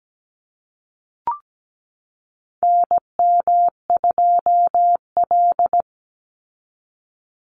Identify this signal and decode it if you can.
Morse code, heard as a keyed tone, about 17 wpm, 700 Hz. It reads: NM2L
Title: Call Signs Repeated 3x - 17wpm